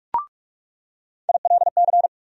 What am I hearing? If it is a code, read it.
Morse code, heard as a keyed tone, about 47 wpm, 700 Hz. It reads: ILC
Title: USA Call Sign Postfixes - 3 Letters 45wpm